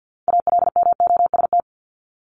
Morse code, heard as a keyed tone, about 50 wpm, 700 Hz. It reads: ALMOST
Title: Top 501-750 Words 50wpm